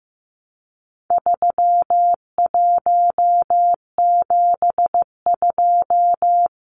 Morse code, heard as a keyed tone, about 15 wpm, 700 Hz. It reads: 3172
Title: Numbers - 4-Digit 15wpm